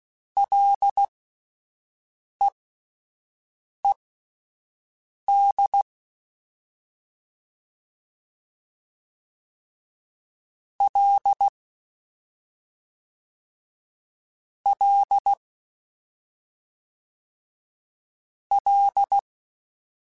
Morse code, heard as a keyed tone, about 16 wpm, spaced as for 5 wpm, 800 Hz. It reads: LEED L L L
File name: Ls / Single L m3s / Single L Prefixe 3x2 Twice @5